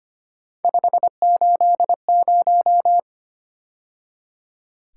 Morse code, heard as a keyed tone, about 29 wpm, 700 Hz. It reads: 580
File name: Speed-Racing - Three Letter-Number Combinations - 25wpm